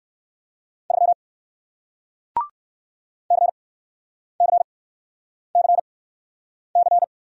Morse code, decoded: 3 CCCC

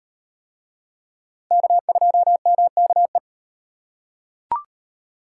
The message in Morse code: K2MKE

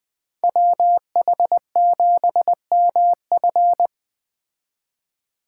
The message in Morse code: WH7MF

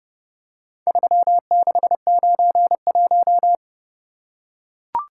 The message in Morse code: 3691

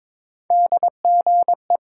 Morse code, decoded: DGE

